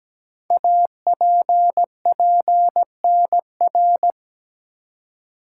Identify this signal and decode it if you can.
Morse code, heard as a keyed tone, about 17 wpm, 700 Hz. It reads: APPNR
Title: Code Groups - 5 Letter:Numbers - 17wpm